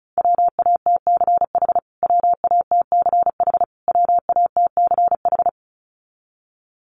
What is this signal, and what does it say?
Morse code, heard as a keyed tone, about 34 wpm, 700 Hz. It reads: WATCH WATCH WATCH
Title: Top 300 Words Repeated 35wpm